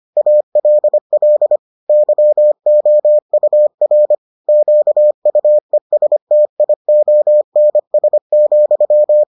Morse code, heard as a keyed tone, about 25 wpm, 600 Hz. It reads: ALL YOUR QUESTIONS,